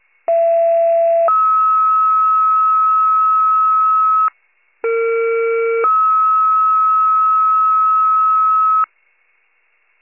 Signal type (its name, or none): two-tone page